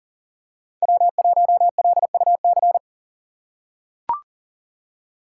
Morse code, heard as a keyed tone, about 40 wpm, 700 Hz. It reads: W1LUC
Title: USA Call Signs - 5 Characters - 4x Character Spacing - 40wpm